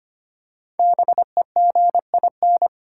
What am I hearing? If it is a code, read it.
Morse code, heard as a keyed tone, about 25 wpm, 700 Hz. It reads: BEGIN
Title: Five Letter Words - 25wpm